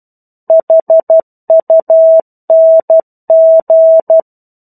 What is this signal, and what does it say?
Morse code, heard as a keyed tone, about 12 wpm, 650 Hz. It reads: HUNG
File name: 01-Chapter-01 12 WPM